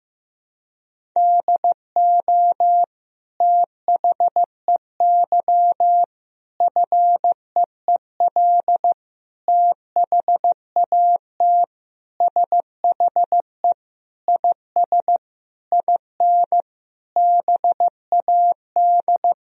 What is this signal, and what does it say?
Morse code, heard as a keyed tone, about 15 wpm, 700 Hz. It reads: DO THEY FEEL THAT SHE IS IN BAD
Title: Sentences from Top 1000 Words - Review Part 1 - 15wpm